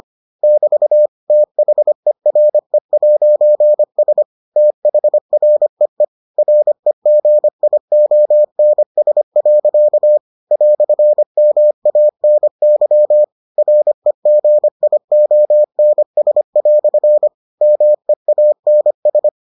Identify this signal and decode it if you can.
Morse code, heard as a keyed tone, about 25 wpm, 600 Hz. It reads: <BT> THERE'S THREE REGIONS. "MANY REGIONS" MEANS